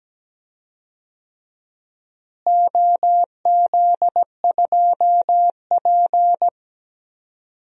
Morse code, heard as a keyed tone, about 17 wpm, 700 Hz. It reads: OZ2P
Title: DXCC-100 Call Signs - 4 Characters - Repeated 3x - 17wpm